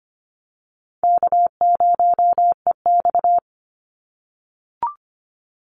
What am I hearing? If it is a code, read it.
Morse code, heard as a keyed tone, about 25 wpm, 700 Hz. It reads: K0EX
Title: Easy Call Signs - 25wpm